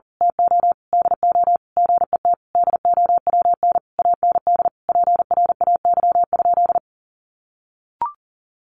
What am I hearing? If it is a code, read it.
Morse code, heard as a keyed tone, about 40 wpm, 700 Hz. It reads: TO DO GET DOWN AND PRAY?
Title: A Sound Of Thunder 40wpm